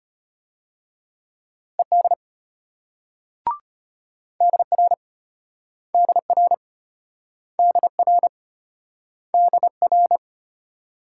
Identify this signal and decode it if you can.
Morse code, heard as a keyed tone, about 33 wpm, 700 Hz. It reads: ED DR DR DR DR